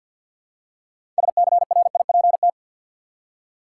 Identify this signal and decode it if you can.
Morse code, heard as a keyed tone, about 53 wpm, 700 Hz. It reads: SCRIPT
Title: Six Letter Words - 50wpm